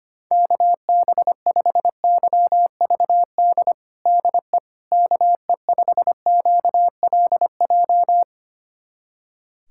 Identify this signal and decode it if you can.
Morse code, heard as a keyed tone, about 25 wpm, 700 Hz. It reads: KB5YVD DE KE5QLJ
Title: USA Callsign DE Callsign - 6 Characters - 25wpm